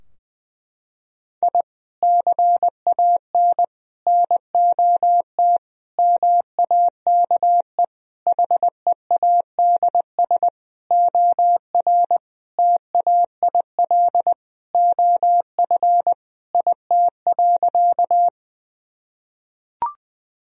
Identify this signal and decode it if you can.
Morse code, heard as a keyed tone, about 21 wpm, 700 Hz. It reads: I CAN NOT MAKE HEADS OR TAIL OF IT.
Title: Sentences from Top 1000 Words - Review Part 3 - 20wpm